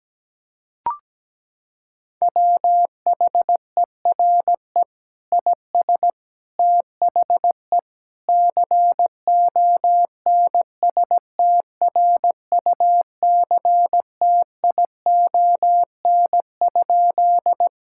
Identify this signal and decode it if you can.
Morse code, heard as a keyed tone, about 17 wpm, 700 Hz. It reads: WHERE IS THE CONSTRUCTION?
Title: Sentences from Top 1600 Words 17wpm